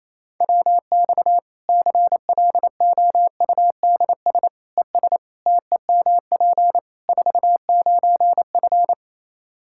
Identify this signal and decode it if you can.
Morse code, heard as a keyed tone, about 28 wpm, 700 Hz. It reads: WX CLOUDS ES TEMP 49F
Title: K1USN SST QSO Practice 28wpm